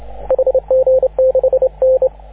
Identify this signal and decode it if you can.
Morse code, heard as a keyed tone, about 25 wpm, 530 Hz, with radio noise. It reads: HG6N
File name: AUDIO2019 / HG6N CW SSB MIX